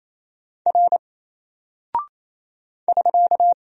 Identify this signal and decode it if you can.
Morse code, heard as a keyed tone, about 28 wpm, 700 Hz. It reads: R <SK>
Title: Single Character (Letter:Number:Punctuation:Prosign) 28wpm